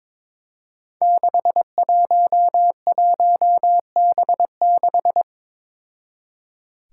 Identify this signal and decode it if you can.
Morse code, heard as a keyed tone, about 22 wpm, 700 Hz. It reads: 611B6
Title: Five Challenging Letter-Number Combinations 22wpm